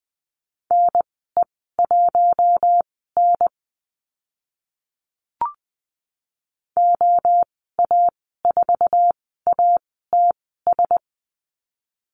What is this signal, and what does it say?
Morse code, heard as a keyed tone, about 20 wpm, spaced as for 14 wpm, 700 Hz. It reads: NE1N OA4ATS
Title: 069 Lesson 12g - Letter d - Call Signs 20wpm